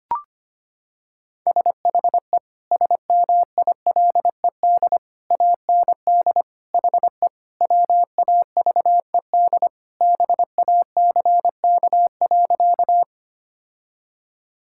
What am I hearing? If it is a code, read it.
Morse code, heard as a keyed tone, about 25 wpm, 700 Hz. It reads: SHE SMILED AND HE WAVED BACK.
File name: practice / 25wpm / Sentences with 2-Word Phrases - 25wpm